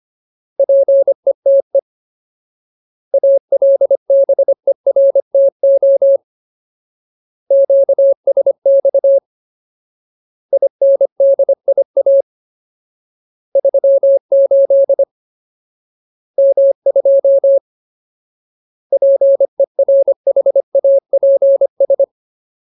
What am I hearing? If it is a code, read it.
Morse code, heard as a keyed tone, about 25 wpm, 550 Hz. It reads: PETE ALBERTO QSX INDIA 38 M2 PERHAPS
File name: Ultimate Word List / 25WPM Ultimate